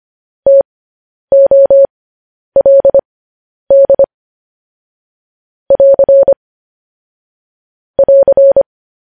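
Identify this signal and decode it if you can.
Morse code, heard as a keyed tone, about 25 wpm, spaced as for 10 wpm, 550 Hz. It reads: TOLD <AR> <AR>
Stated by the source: Morse code news headlines practice transmission.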